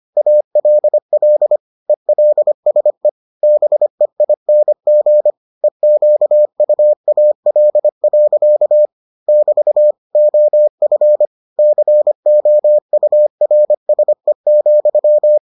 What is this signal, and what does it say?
Morse code, heard as a keyed tone, about 25 wpm, 600 Hz. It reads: ALL ELSE BEING EQUAL. <BT> OF COURSE,